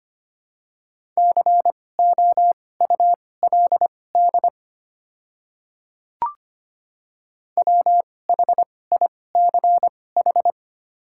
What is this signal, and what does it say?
Morse code, heard as a keyed tone, about 25 wpm, spaced as for 18 wpm, 690 Hz. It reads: COULD WHICH
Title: 2x Character Spacing - Top 200 Words - 25wpm